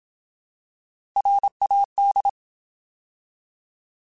Morse code, heard as a keyed tone, about 27 wpm, 800 Hz. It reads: RAD
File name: Alphabetical / Alphabetical mp3s / Rs @25